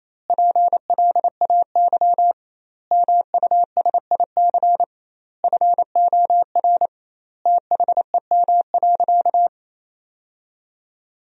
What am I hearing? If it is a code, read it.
Morse code, heard as a keyed tone, about 27 wpm, 700 Hz. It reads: PLAY MUSIC FOR THEM.
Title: 2x Word-Spacing - Sentences from Top 300 Words - 28wpm